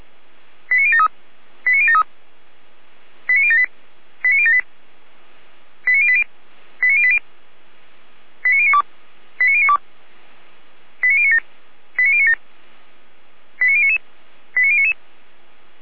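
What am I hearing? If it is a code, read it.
Five-tone selective call (ZVEI), heard as quick runs of five tones, tones of about 70 ms. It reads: 89073 89073 89078 89078 89080 89080 89092 89092 89097 89097 89099 89099
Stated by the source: Five-tone ZVEI selective call received on SDR, from feuerwehr-orlamuende.org (Feuerwehr Orlamünde, fire brigade).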